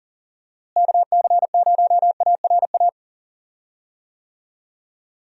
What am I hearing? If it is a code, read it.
Morse code, heard as a keyed tone, about 40 wpm, 700 Hz. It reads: KC0ARA
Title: USA Call Signs - 6 Characters - Repeated 2x - 40wpm